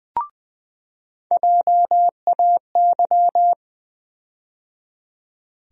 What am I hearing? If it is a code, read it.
Morse code, heard as a keyed tone, about 20 wpm, 700 Hz. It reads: JAY
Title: USA Call Sign Postfixes - 3 Letters 20wpm